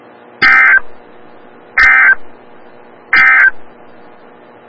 Emergency Alert System (SAME) tones, heard as harsh buzzing bursts. It is heard from 0.3 to 3.6 s.
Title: Des Moines, IA NOAA Weather Radio - WXL57